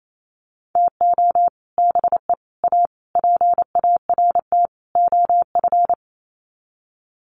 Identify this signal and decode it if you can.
Morse code, heard as a keyed tone, about 28 wpm, 700 Hz. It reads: TO BE A PART OF
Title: Sets of 5 Words 28wpm